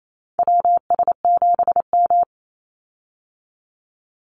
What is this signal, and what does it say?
Morse code, heard as a keyed tone, about 28 wpm, 700 Hz. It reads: WS7M
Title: Call Signs - 28wpm